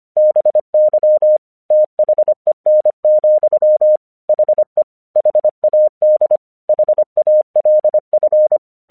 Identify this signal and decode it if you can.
Morse code, heard as a keyed tone, about 24 wpm, 610 Hz. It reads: BY THEN, HE HAD HALF
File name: 1 English 25wpm / AQC24